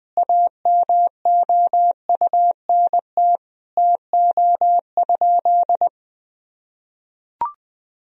Morse code, heard as a keyed tone, about 21 wpm, 700 Hz. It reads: AMOUNT TO?
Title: Sentences from Top 1000 Words - Review Part 4 - 20wpm